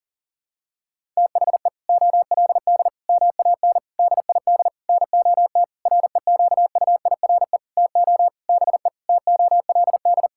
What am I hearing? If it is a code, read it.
Morse code, heard as a keyed tone, about 40 wpm, 700 Hz. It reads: THE OLD MAN DID NOT REQUIRE TO BE TOLD
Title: Sentences from Top 1600 Words 40wpm